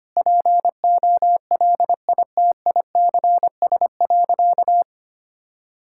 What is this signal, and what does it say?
Morse code, heard as a keyed tone, about 25 wpm, 700 Hz. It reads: POLITICS.
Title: Sentences from Top 1000 Words - Review Part 6 - 25wpm